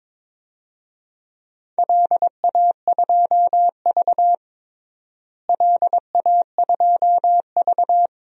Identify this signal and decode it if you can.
Morse code, heard as a keyed tone, about 21 wpm, 700 Hz. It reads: LA2V LA2V
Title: DXCC-100 Call Signs - 4 Characters - Repeated 2x - 22wpm